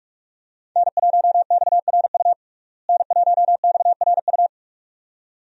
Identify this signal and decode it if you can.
Morse code, heard as a keyed tone, about 47 wpm, 700 Hz. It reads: N1XRU N1XRU
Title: USA Call Signs - 5 Characters - Repeated 2x - 45wpm